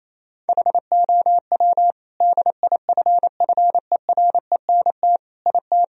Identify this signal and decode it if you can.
Morse code, heard as a keyed tone, about 27 wpm, 700 Hz. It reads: HOW DIFFERENT IT
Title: Sentences from Top 1000 Words - Review Part 3 - 28wpm